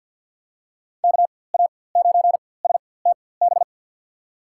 Morse code, heard as a keyed tone, about 47 wpm, spaced as for 23 wpm, 700 Hz. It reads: KA9STB
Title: USA Call Signs - 4x Character Spacing - 50wpm